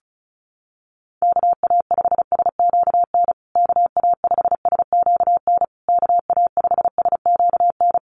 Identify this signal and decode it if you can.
Morse code, heard as a keyed tone, about 34 wpm, 700 Hz. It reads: KA5SQN KA5SQN KA5SQN
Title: USA Call Signs Repeated 3x 35wpm